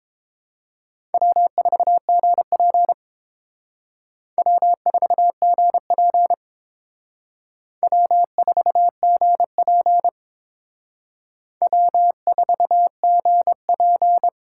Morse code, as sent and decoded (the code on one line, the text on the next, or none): .-- ....- --. .--. / .-- ....- --. .--. / .-- ....- --. .--. / .-- ....- --. .--.
W4GP W4GP W4GP W4GP